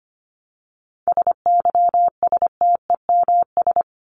Morse code, heard as a keyed tone, about 25 wpm, 700 Hz. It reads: SYSTEMS
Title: ES:S Suffix - Top 100 Words with Base Words 25wpm